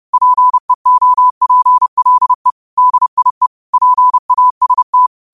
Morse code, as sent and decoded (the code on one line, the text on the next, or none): .--. --- .--. .-.. / -.. .. / .--. .- ... -
PEOPLE DIE PAST